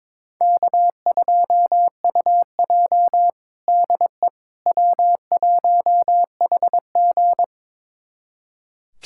Morse code, heard as a keyed tone, about 22 wpm, 700 Hz. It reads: K2UJ DE W1HG
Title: USA Callsign DE Callsign - 4 Characters - 22wpm